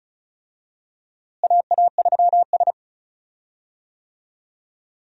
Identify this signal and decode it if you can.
Morse code, heard as a keyed tone, about 36 wpm, 700 Hz. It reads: AA3S
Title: USA Call Signs - 4 Characters - Repeated 2x - 35wpm